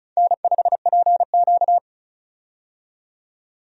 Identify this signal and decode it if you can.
Morse code, heard as a keyed tone, about 36 wpm, 700 Hz. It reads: N5PQ
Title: USA Call Signs - 4 Characters - 35wpm